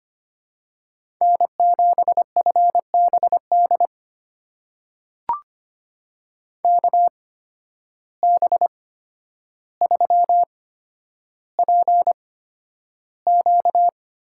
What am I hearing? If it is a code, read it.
Morse code, heard as a keyed tone, about 25 wpm, spaced as for 11 wpm, 700 Hz. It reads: N7FBD KB3PQ
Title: USA Call Signs - 5 Characters - 8x Character Spacing - 25wpm